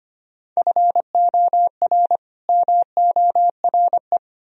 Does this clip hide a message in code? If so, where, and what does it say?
Yes: Morse code from 0.6 to 4.2 s, heard as a keyed tone, about 33 wpm, 700 Hz. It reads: FOR MORE